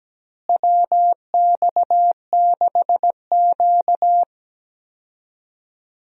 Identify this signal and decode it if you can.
Morse code, heard as a keyed tone, about 17 wpm, 700 Hz. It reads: WX6Q
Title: USA Call Signs - 4 Characters - Repeated 2x - 17wpm